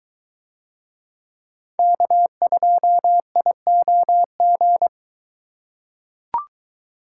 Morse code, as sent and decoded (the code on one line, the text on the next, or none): -.- ..--- .. --- --.
K2IOG